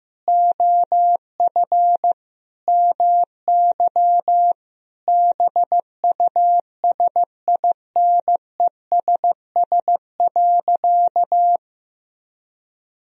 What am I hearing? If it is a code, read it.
Morse code, heard as a keyed tone, about 15 wpm, 700 Hz. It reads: OF MY BUSINESS.
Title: Sentences from Top 1000 Words - Review Part 2 - 15wpm